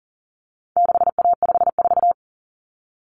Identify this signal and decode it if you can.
Morse code, heard as a keyed tone, about 40 wpm, 700 Hz. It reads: 6A54
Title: Common Cut Numbers - 4-Digit 40wpm